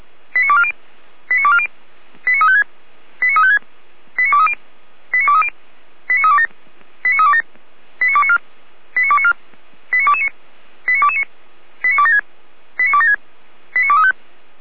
Five-tone selective call (ZVEI), heard as quick runs of five tones, tones of about 70 ms. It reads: 89240 89240 89356 89356 89230 89230 89327 89327 89294 89294 89229 89229 89276 89276 89235